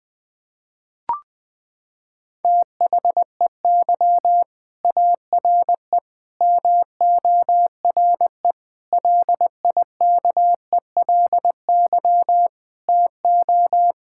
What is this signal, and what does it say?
Morse code, heard as a keyed tone, about 20 wpm, 700 Hz. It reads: THEY ARE MORE LIKELY TO
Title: Sentences with 4-Word Phrases - 20wpm